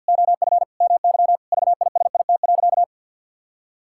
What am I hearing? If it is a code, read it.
Morse code, heard as a keyed tone, about 50 wpm, 700 Hz. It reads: OF MY VISIT.
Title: Sentences from Top 1000 Words - Review Part 6 - 50wpm